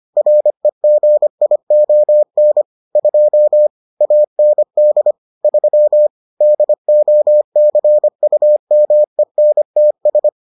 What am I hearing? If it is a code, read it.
Morse code, heard as a keyed tone, about 25 wpm, 600 Hz. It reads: REGION 2 AND 3 DOCUMENTS